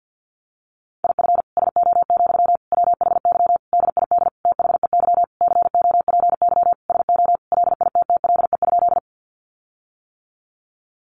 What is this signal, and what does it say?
Morse code, heard as a keyed tone, about 50 wpm, 700 Hz. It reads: IF SO, WHY DID THEY COPY SO LITTLE?
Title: Sentences from Top 1400 Words 50wpm